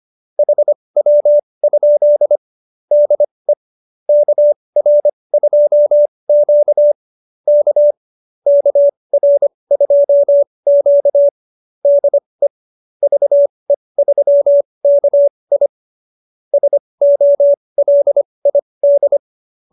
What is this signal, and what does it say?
Morse code, heard as a keyed tone, about 25 wpm, spaced as for 20 wpm, 590 Hz. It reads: HW? DE KR2Q K KR2Q DE VE3KI SOLID